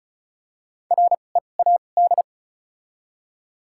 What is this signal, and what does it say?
Morse code, heard as a keyed tone, about 36 wpm, spaced as for 25 wpm, 700 Hz. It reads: READ
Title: 2x Character Spacing - Top 300 Words - 35wpm